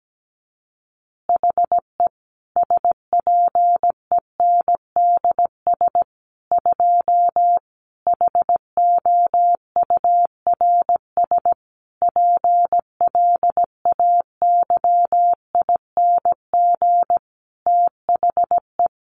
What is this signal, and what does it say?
Morse code, heard as a keyed tone, about 17 wpm, 700 Hz. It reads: HE SPENDS 2 HOURS PLAYING THE
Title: Additional Easy Phrases 17wpm